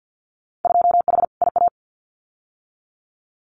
Morse code, heard as a keyed tone, about 50 wpm, 700 Hz. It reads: 2H IA
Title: Winter Field Day Exchanges Encore 50wpm